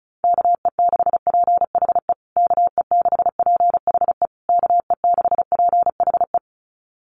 Morse code, heard as a keyed tone, about 34 wpm, 700 Hz. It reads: KE6PHE KE6PHE KE6PHE
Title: USA Call Signs Repeated 3x 35wpm